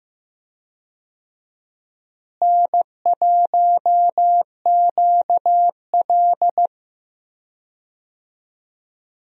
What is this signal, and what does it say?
Morse code, heard as a keyed tone, about 15 wpm, 700 Hz. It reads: N1QL
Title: USA Call Signs - 4 Characters - Repeated 2x - 15wpm